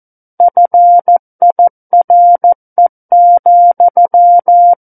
Morse code, heard as a keyed tone, about 14 wpm, 700 Hz. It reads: FIRE,
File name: Book One / 05-Chapter 5 - 14 WPM